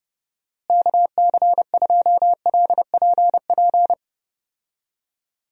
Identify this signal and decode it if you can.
Morse code, heard as a keyed tone, about 30 wpm, 700 Hz. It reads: KC2LPP
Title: USA Call Signs - 6 Characters - Repeated 3x - 30wpm